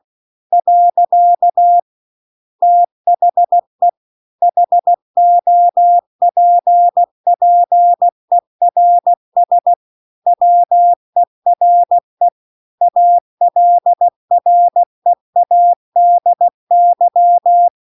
Morse code, heard as a keyed tone, about 16 wpm, 700 Hz. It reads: . THE HOPPERS WERE ALREADY
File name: 16WPM